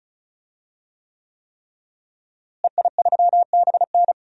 Morse code, heard as a keyed tone, about 34 wpm, 700 Hz. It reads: EI3BN
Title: DXCC-100 Call Signs - 5 Characters - Repeated 3x - 35wpm